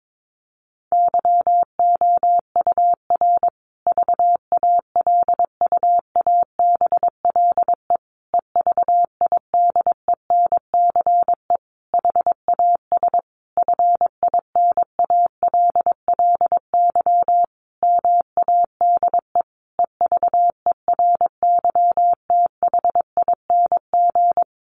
Morse code, heard as a keyed tone, about 22 wpm, 700 Hz. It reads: YOUR VALUABLE EVIDENCE HAS FINALLY MADE EVERYTHING
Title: Sentences from Top 1300 Words 22wpm